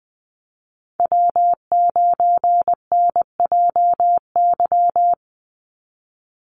Morse code, heard as a keyed tone, about 20 wpm, 700 Hz. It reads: W9NJY
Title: Call Signs Repeated 3x - 20wpm